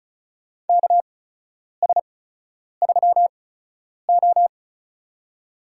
Morse code, heard as a keyed tone, about 36 wpm, spaced as for 10 wpm, 700 Hz. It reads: KS3O
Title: USA Call Signs - 8x Character Spacing - 35wpm